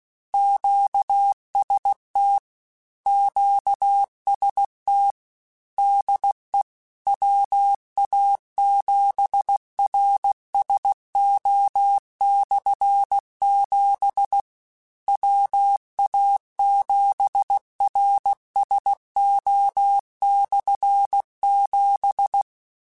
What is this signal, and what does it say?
Morse code, heard as a keyed tone, about 16 wpm, 800 Hz. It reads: QST QST DE WA7RSO/7 WA7RSO/7
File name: Combinations / Combinations-mp3s / Combinations@15